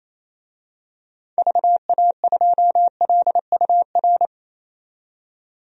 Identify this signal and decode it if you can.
Morse code, heard as a keyed tone, about 28 wpm, 700 Hz. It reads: VA2LUR
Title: DXCC-100 Call Signs - 6 Characters - 28wpm